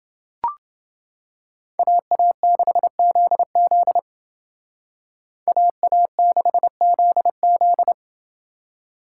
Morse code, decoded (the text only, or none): AA6ZZ AA6ZZ